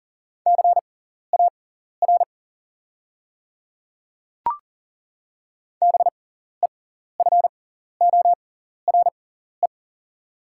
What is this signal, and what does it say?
Morse code, heard as a keyed tone, about 40 wpm, spaced as for 14 wpm, 690 Hz. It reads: CAR BEFORE
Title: 6x Character Spacing - Top 300 Words - 40wpm